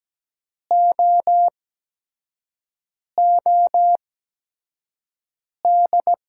Morse code, heard as a keyed tone, about 17 wpm, spaced as for 5 wpm, 700 Hz. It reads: OOD
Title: 8x Character Spacing - Top 500 Words - 17wpm